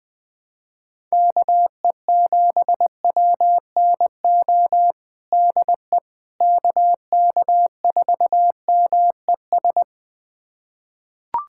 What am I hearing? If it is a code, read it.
Morse code, heard as a keyed tone, about 21 wpm, 700 Hz. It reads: KE7WNO DE KK4MES
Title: USA Callsign DE Callsign - 6 Characters - 20wpm